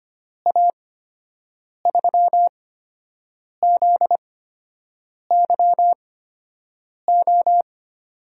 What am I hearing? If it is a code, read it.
Morse code, heard as a keyed tone, about 25 wpm, spaced as for 7 wpm, 700 Hz. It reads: A3ZYO
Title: USA Call Signs - 6 Characters - 8x Character Spacing - 25wpm